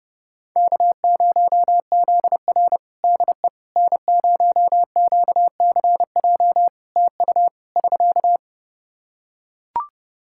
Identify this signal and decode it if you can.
Morse code, heard as a keyed tone, about 32 wpm, 700 Hz. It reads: K0ZR DE N0QCJ TU <SK>
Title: Practice QSO - Exchange 4 - Ending - 30wpm